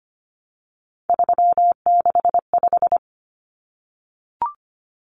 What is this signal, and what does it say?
Morse code, heard as a keyed tone, about 25 wpm, 700 Hz. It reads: 365
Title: Common Cut Numbers - 3-Digit 25wpm